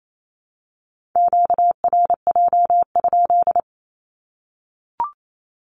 Morse code, heard as a keyed tone, about 28 wpm, 700 Hz. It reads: QRJ?
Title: Q-Codes Repeated 3x 28wpm